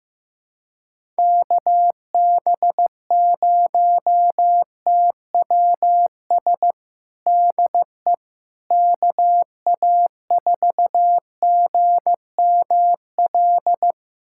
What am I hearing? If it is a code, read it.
Morse code, heard as a keyed tone, about 16 wpm, 700 Hz. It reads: KB0TWS DE KA4GML
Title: USA Callsign DE Callsign - 6 Characters - 15wpm